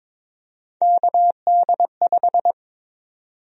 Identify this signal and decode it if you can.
Morse code, heard as a keyed tone, about 22 wpm, 700 Hz. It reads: KD5
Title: USA Call Sign Prefixes - Repeated 2x - 22wpm